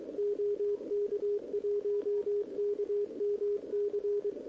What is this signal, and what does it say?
Morse code, heard as a keyed tone, about 23 wpm, 410 Hz, with radio noise. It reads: OK1KMC
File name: OK QRP 2026 / OKQRP2026